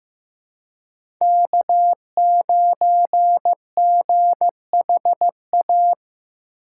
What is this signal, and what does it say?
Morse code, heard as a keyed tone, about 15 wpm, 700 Hz. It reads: K9GHA
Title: USA Call Signs - 5 Characters - 4x Character Spacing - 15wpm